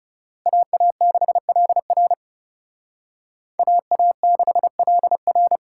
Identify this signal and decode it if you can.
Morse code, heard as a keyed tone, about 38 wpm, 700 Hz. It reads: AA6LR AA6LR